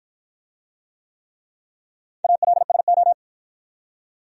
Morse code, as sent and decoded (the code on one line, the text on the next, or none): .- .-.. ... ---
ALSO